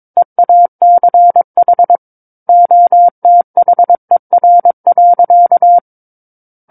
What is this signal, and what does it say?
Morse code, heard as a keyed tone, about 22 wpm, 700 Hz. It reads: EACH OTHER.
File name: Book One / 13-Chapter 13 - 22 WPM